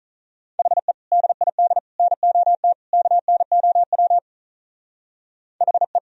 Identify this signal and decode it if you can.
Morse code, heard as a keyed tone, about 44 wpm, 700 Hz. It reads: HE DID NOT KNOW HE